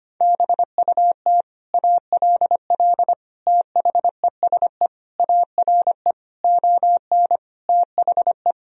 Morse code, heard as a keyed tone, about 25 wpm, 700 Hz. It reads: BUT ALL THESE ARE ON THE
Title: Sentences from Top 1000 Words - Review Part 3 - 25wpm